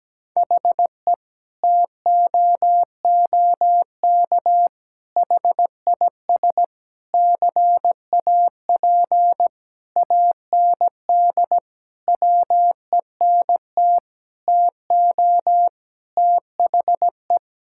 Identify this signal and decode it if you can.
Morse code, heard as a keyed tone, about 17 wpm, 700 Hz. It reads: HE TOOK HIS CAP AND WENT TO THE
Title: Sentences from Top 1600 Words 17wpm